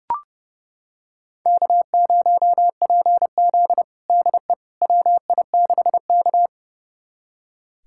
Morse code, heard as a keyed tone, about 32 wpm, 700 Hz. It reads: K0PZ DE WI6K
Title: USA Callsign DE Callsign - 4 Characters - 30wpm